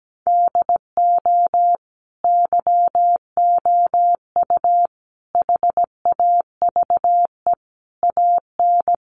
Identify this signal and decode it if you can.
Morse code, heard as a keyed tone, about 17 wpm, 700 Hz. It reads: DO YOU HAVE AN
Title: Sentences from Top 1500 Words 17wpm